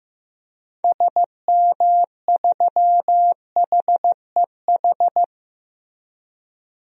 Morse code, heard as a keyed tone, about 15 wpm, 700 Hz. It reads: SM3HEH
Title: DXCC-100 Call Signs - 6 Characters - Repeated 2x - 15wpm